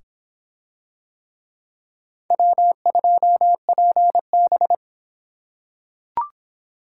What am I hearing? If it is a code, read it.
Morse code, heard as a keyed tone, about 22 wpm, 700 Hz. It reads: W2PB